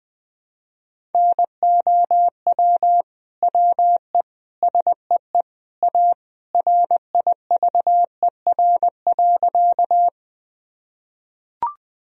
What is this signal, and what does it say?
Morse code, heard as a keyed tone, about 20 wpm, 700 Hz. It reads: NOW WE SEE A RIVER.
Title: Sentences from Top 1000 Words - Review Part 1 - 20wpm